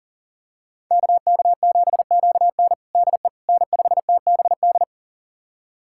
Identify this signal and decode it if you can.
Morse code, heard as a keyed tone, about 40 wpm, 700 Hz. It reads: KK7QN DE N5TBD
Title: USA Callsign DE Callsign - 5 Characters - 40wpm